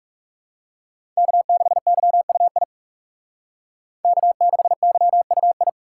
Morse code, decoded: K6YUI K6YUI

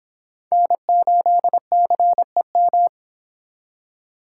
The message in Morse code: N8CEM

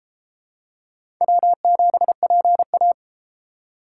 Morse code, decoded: W7PA

